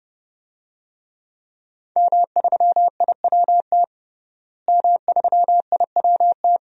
Morse code, heard as a keyed tone, about 30 wpm, 700 Hz. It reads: M3IWT M3IWT
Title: DXCC-100 Call Signs - 5 Characters - Repeated 3x - 30wpm